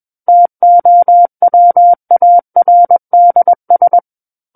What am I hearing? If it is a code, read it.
Morse code, heard as a keyed tone, about 21 wpm, 700 Hz. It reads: TOWARDS
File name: Book One / 12-Chapter 12 - 21 WPM